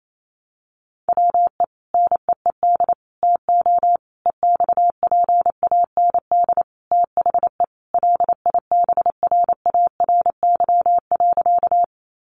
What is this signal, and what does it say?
Morse code, heard as a keyed tone, about 28 wpm, 700 Hz. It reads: WE NEED TO EXPAND THE LIBRARY.
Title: Sentences from Top 1400 Words 28wpm